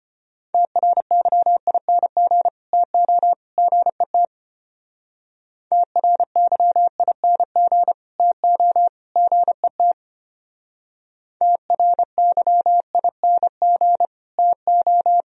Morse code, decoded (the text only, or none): TRYING TO GET TRYING TO GET TRYING TO